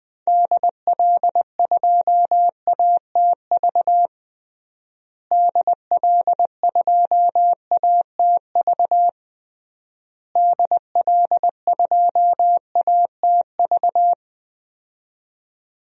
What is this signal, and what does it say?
Morse code, heard as a keyed tone, about 20 wpm, 700 Hz. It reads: DL2ATV DL2ATV DL2ATV
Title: DXCC-100 Call Signs - 6 Characters - Repeated 3x - 20wpm